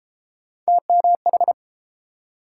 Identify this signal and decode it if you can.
Morse code, heard as a keyed tone, about 30 wpm, 700 Hz. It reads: TMH